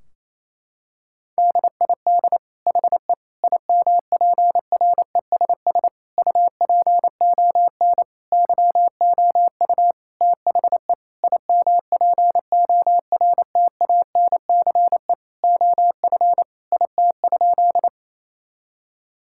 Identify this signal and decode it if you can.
Morse code, heard as a keyed tone, about 28 wpm, 700 Hz. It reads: DID HE IMPRESS UPON YOU THE IMPORTANCE OF IT?
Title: Sentences from Top 1600 Words 28wpm